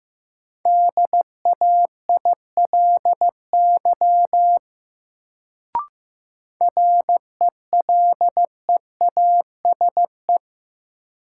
Morse code, heard as a keyed tone, about 15 wpm, 700 Hz. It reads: DAILY RELEASE
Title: Sentences from Top 1100 Words - New Word Drill - 15wpm